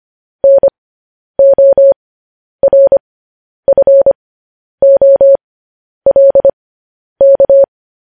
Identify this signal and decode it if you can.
Morse code, heard as a keyed tone, about 25 wpm, spaced as for 10 wpm, 550 Hz. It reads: NORFOLK